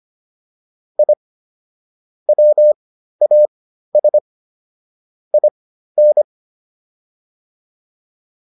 Morse code, heard as a keyed tone, about 25 wpm, spaced as for 13 wpm, 600 Hz. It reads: I WAS IN